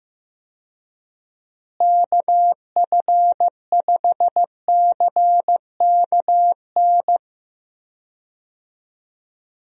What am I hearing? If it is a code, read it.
Morse code, heard as a keyed tone, about 14 wpm, 700 Hz. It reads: KF5CKN
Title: USA Call Signs - 6 Characters - Repeated 2x - 15wpm